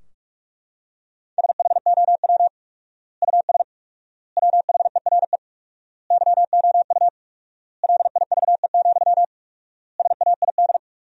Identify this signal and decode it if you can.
Morse code, heard as a keyed tone, about 42 wpm, 700 Hz. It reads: SHOW US WHERE YOU LIVE, SAID
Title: 4x Word-Spacing - Sentences from Top 200 Words - 45wpm